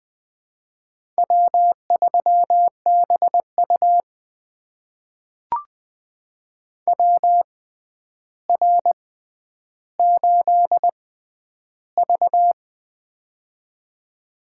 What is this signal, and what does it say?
Morse code, heard as a keyed tone, about 20 wpm, spaced as for 10 wpm, 700 Hz. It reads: W3BU WR8V
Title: USA Call Signs - 4 Characters - 6x Character Spacing - 20wpm